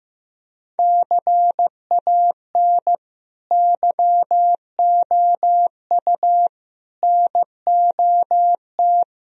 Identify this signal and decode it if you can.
Morse code, heard as a keyed tone, about 16 wpm, 700 Hz. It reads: CAN YOU NOT
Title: Sentences from Top 1000 Words - Review Part 2 - 15wpm